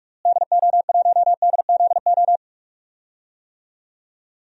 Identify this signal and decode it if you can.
Morse code, heard as a keyed tone, about 46 wpm, 700 Hz. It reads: DO1DZO
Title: DXCC-100 Call Signs - 6 Characters - Repeated 2x - 45wpm